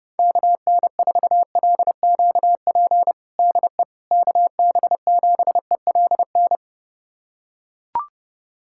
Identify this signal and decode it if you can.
Morse code, heard as a keyed tone, about 30 wpm, 700 Hz. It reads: KN4LQP DE KB7ELN
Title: USA Callsign DE Callsign 30wpm